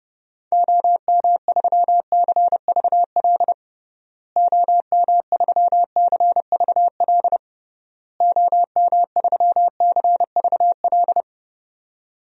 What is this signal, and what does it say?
Morse code, heard as a keyed tone, about 30 wpm, 700 Hz. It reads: OM3CVL OM3CVL OM3CVL
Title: DXCC-100 Call Signs - 6 Characters - Repeated 3x - 30wpm